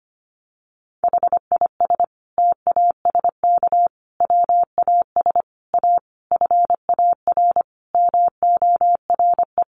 Morse code, heard as a keyed tone, about 25 wpm, 700 Hz. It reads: HIS TASK WAS A FAR MORE
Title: Sentences from Top 1200 Words 25wpm